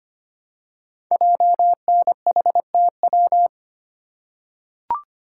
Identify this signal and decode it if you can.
Morse code, heard as a keyed tone, about 25 wpm, 700 Hz. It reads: JNHTW
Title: Code Groups - 5 Letters - 25wpm